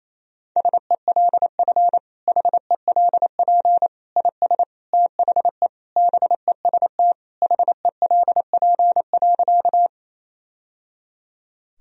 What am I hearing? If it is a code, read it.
Morse code, heard as a keyed tone, about 28 wpm, 700 Hz. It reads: SELF HELP IS THE BEST HELP.
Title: Sentences from Top 1000 Words - Review Part 1 - 28wpm